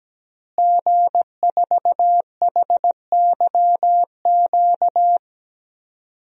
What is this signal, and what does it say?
Morse code, heard as a keyed tone, about 17 wpm, 700 Hz. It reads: G4HYQ
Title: DXCC-100 Call Signs - 5 Characters - Repeated 2x - 17wpm